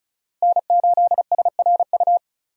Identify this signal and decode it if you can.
Morse code, heard as a keyed tone, about 36 wpm, 700 Hz. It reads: N8SRU
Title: USA Call Signs - 5 Characters - Repeated 2x - 35wpm